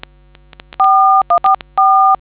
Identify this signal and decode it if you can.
DTMF, heard as telephone touch tones, tones of about 260 ms. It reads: 4144